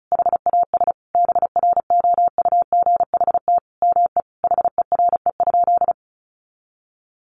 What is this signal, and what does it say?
Morse code, heard as a keyed tone, about 34 wpm, 700 Hz. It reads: HAS BROUGHT ME HERE?
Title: Sentences from Top 1200 Words 35wpm